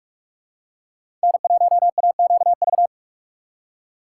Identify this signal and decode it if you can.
Morse code, heard as a keyed tone, about 46 wpm, 700 Hz. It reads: N1AQV
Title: USA Call Signs - 5 Characters - 3x Character Spacing - 45wpm